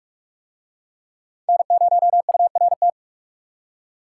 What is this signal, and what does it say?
Morse code, heard as a keyed tone, about 44 wpm, 700 Hz. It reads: N0URT